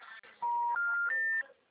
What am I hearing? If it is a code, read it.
Special information tone.